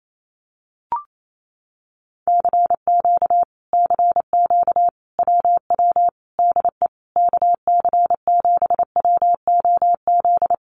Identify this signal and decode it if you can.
Morse code, heard as a keyed tone, about 28 wpm, 700 Hz. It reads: CQ CQ WW DE KC7WOZ
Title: Common QSO Phrases Encore 28wpm